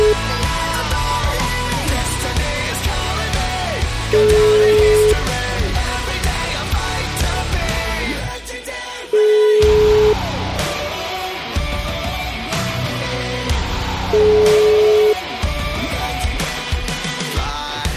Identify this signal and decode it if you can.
Ringback tone (Europe, 425 Hz), with noise.